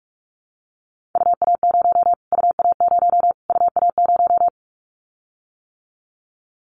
Morse code, heard as a keyed tone, about 44 wpm, 700 Hz. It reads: UA0 UA0 UA0
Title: Callsign Prefixes for Top 25 DXCC - Repeated 3x - 45wpm